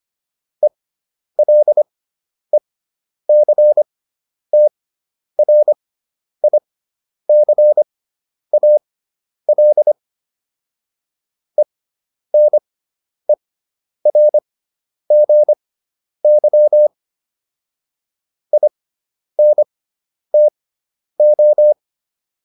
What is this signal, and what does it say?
Morse code, heard as a keyed tone, about 25 wpm, spaced as for 10 wpm, 600 Hz. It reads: ELECTRICAL ENERGY INTO